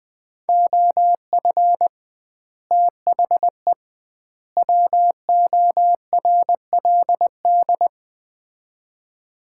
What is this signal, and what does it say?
Morse code, heard as a keyed tone, about 20 wpm, 700 Hz. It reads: OF THE WORLD
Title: 2x Word-Spacing - Sets of 3 Words - 20wpm